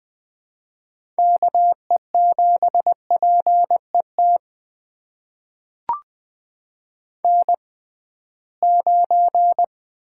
Morse code, heard as a keyed tone, about 20 wpm, spaced as for 15 wpm, 700 Hz. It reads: KE7PET N9
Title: USA Call Signs - 6x Character Spacing - 20wpm